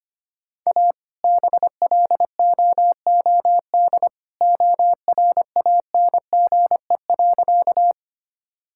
Morse code, heard as a keyed tone, about 25 wpm, 700 Hz. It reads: A BLOOD ORANGE.
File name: practice / 25wpm / All-Age Appropriate Jokes - 25wpm